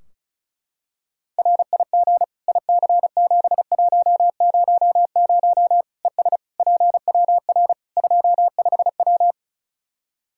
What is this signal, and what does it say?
Morse code, heard as a keyed tone, about 36 wpm, 700 Hz. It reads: RIG IC7100 ES PWR 25W
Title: K1USN SST QSO Practice 35wpm